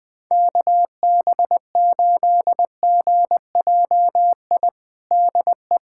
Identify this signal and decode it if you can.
Morse code, heard as a keyed tone, about 20 wpm, 700 Hz. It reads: KB8GJI DE
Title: USA Callsign DE Callsign 20wpm